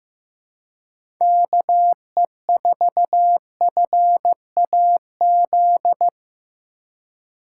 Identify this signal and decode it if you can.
Morse code, heard as a keyed tone, about 15 wpm, 700 Hz. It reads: KE4FAZ
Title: USA Call Signs - 2x Character Spacing - 15wpm